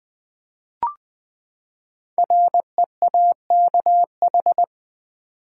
Morse code, heard as a keyed tone, about 21 wpm, 700 Hz. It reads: REAKH
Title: Code Groups - 5 Letter:Numbers - 20wpm